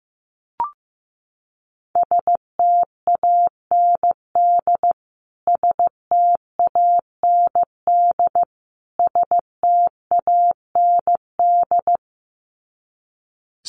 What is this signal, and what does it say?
Morse code, heard as a keyed tone, about 15 wpm, 700 Hz. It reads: STAND STAND STAND
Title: Top 300 Words Repeated 15wpm